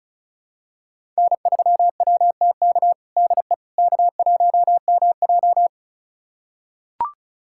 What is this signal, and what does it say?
Morse code, heard as a keyed tone, about 36 wpm, 700 Hz. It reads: N3WTK DE K1MJ
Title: USA Callsign DE Callsign 35wpm